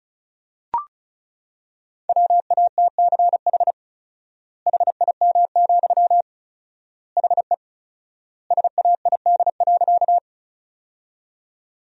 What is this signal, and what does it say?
Morse code, heard as a keyed tone, about 36 wpm, 700 Hz. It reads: WATCH HIM, HE SAID.
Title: 4x Word-Spacing - Sentences from Top 300 Words - 35wpm